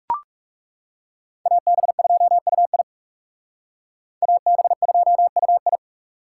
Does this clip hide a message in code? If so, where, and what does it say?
Yes: Morse code from 1.5 to 5.8 s, heard as a keyed tone, about 38 wpm, 700 Hz. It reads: AB2UI AB2UI